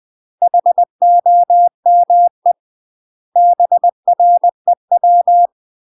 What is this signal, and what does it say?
Morse code, heard as a keyed tone, about 20 wpm, 700 Hz. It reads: HOME BREW
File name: UK MASTER QSOs TRAINING / 20WPM / UK MASTER QSOs 0064